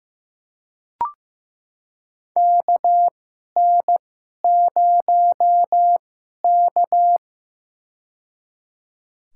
Morse code, heard as a keyed tone, about 15 wpm, spaced as for 11 wpm, 700 Hz. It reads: KN0K
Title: USA Call Signs - 4 Characters - 2x Character Spacing - 15wpm